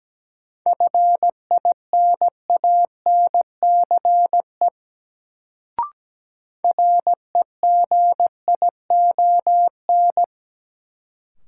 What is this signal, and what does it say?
Morse code, heard as a keyed tone, about 17 wpm, 700 Hz. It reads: FINANCE REGION
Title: Sentences from Top 1500 Words - New Word Drill - 17wpm